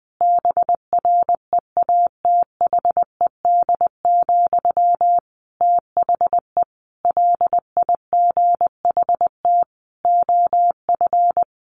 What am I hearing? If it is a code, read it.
Morse code, heard as a keyed tone, about 19 wpm, 700 Hz. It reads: BREATHED, THE LIGHT OF
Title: A Sound Of Thunder 20wpm